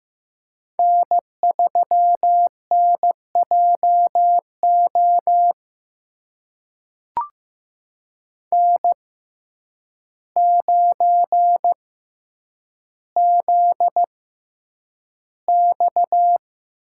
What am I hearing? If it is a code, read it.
Morse code, heard as a keyed tone, about 15 wpm, spaced as for 8 wpm, 700 Hz. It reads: N3NJO N9ZX
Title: USA Call Signs - 5 Characters - 6x Character Spacing - 15wpm